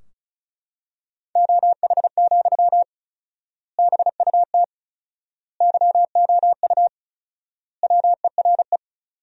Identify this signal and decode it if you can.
Morse code, heard as a keyed tone, about 34 wpm, 700 Hz. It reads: OH, BUT YOU WERE
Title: 4x Word-Spacing - Sentences from Top 500 Words - 35wpm